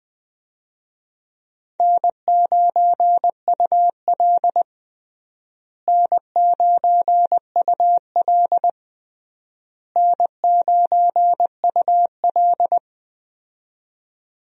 Morse code, heard as a keyed tone, about 20 wpm, 700 Hz. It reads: N9UL N9UL N9UL
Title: USA Call Signs - 4 Characters - Repeated 3x - 20wpm